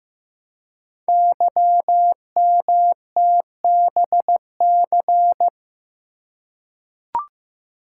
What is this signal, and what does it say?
Morse code, heard as a keyed tone, about 15 wpm, 700 Hz. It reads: YMTBC
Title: Code Groups - 5 Letters - 15wpm